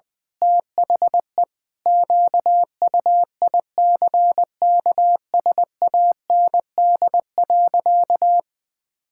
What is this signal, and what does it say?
Morse code, heard as a keyed tone, about 20 wpm, 700 Hz. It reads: THE QUICKSAND.
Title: Pangram Challenge v2 - 20wpm